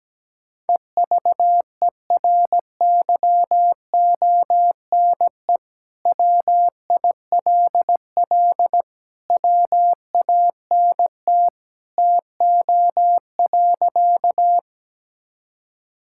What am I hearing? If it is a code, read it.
Morse code, heard as a keyed tone, about 17 wpm, 700 Hz. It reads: EVERYONE WILL WANT TO.
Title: Sentences from Top 1000 Words - Review Part 5 - 17wpm